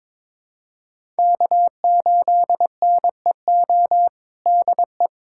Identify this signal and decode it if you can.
Morse code, heard as a keyed tone, about 22 wpm, 700 Hz. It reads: K8NEO DE
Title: USA Callsign DE Callsign 22wpm